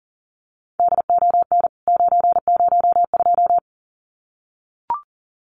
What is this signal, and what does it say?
Morse code, heard as a keyed tone, about 40 wpm, 700 Hz. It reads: DON 902
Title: CWT Exchange 40wpm